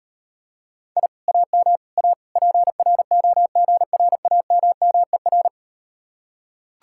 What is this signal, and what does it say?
Morse code, heard as a keyed tone, about 38 wpm, 700 Hz. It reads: I AM A PROGRAMMER